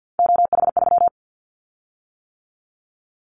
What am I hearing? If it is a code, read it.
Morse code, heard as a keyed tone, about 50 wpm, 700 Hz. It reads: OH3
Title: Callsign Prefixes for Top 50 DXCC - 50wpm